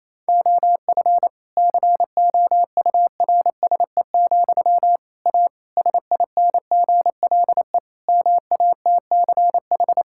Morse code, heard as a keyed tone, about 28 wpm, 700 Hz. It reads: OF COURSE, A SINGLE MATCH
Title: Sentences from Top 1600 Words 28wpm